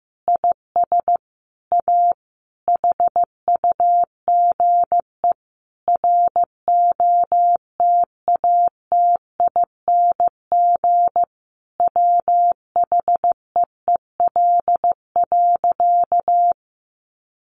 Morse code, heard as a keyed tone, about 15 wpm, 700 Hz. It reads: IS A HUGE ROTATING WHEEL.
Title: Additional Easy Phrases 15wpm